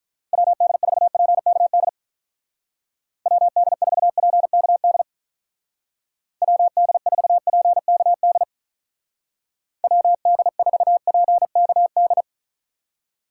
Morse code, heard as a keyed tone, about 44 wpm, 700 Hz. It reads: WD4PKD WD4PKD WD4PKD WD4PKD
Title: Speed Racing - USA Call Signs 35wpm